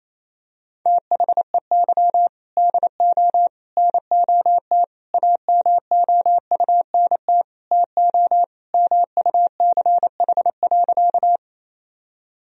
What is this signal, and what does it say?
Morse code, heard as a keyed tone, about 28 wpm, 700 Hz. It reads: THEY DO NOT AMOUNT TO MUCH.
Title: Sentences from Top 1000 Words - Review Part 4 - 28wpm